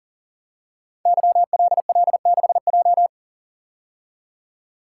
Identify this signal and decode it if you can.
Morse code, heard as a keyed tone, about 43 wpm, 700 Hz. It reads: YLL6J
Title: Code Groups - 5 Letter:Numbers - 40wpm